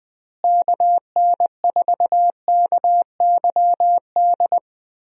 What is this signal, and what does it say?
Morse code, heard as a keyed tone, about 20 wpm, 700 Hz. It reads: KN4KYD
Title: USA Call Signs - 6 Characters - Repeated 2x - 20wpm